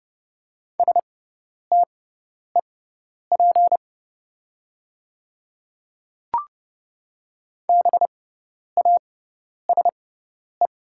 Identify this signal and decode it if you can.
Morse code, heard as a keyed tone, about 30 wpm, spaced as for 10 wpm, 690 Hz. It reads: STEP BASE
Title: 6x Character Spacing - Top 500 Words - 30wpm